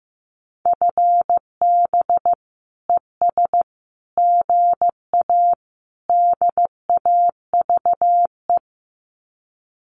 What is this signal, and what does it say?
Morse code, heard as a keyed tone, about 15 wpm, 700 Hz. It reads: FB ES GA DAVE
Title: Common QSO Phrases Encore 15wpm